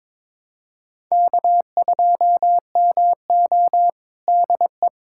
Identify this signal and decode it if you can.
Morse code, heard as a keyed tone, about 22 wpm, 700 Hz. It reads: K2MO DE
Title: USA Callsign DE Callsign - 4 Characters - 22wpm